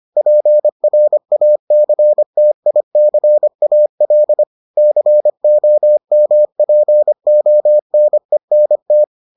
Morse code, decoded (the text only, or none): PRACTICAL COMPONENT